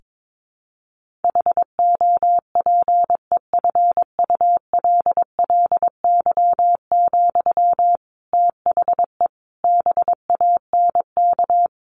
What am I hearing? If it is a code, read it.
Morse code, heard as a keyed tone, about 22 wpm, 700 Hz. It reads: HOPEFULLY, THE BANK
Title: Sentences from Top 1500 Words 22wpm